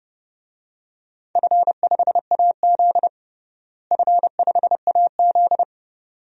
Morse code, heard as a keyed tone, about 30 wpm, 700 Hz. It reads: F5AZ F5AZ
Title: DXCC-100 Call Signs - 4 Characters - Repeated 2x - 30wpm